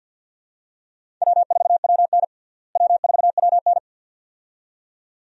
Morse code, heard as a keyed tone, about 53 wpm, 700 Hz. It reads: W4WN W4WN
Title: USA Call Signs - 4 Characters - Repeated 2x - 50wpm